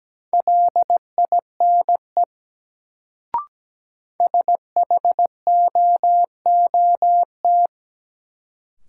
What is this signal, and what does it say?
Morse code, heard as a keyed tone, about 17 wpm, 700 Hz. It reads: LINE SHOOT
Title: Sentences from Top 1200 Words - New Word Drill Alt - 17wpm